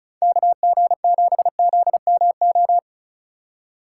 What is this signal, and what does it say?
Morse code, heard as a keyed tone, about 36 wpm, 700 Hz. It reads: KG7ZMO
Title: USA Call Signs - 6 Characters - 3x Character Spacing - 35wpm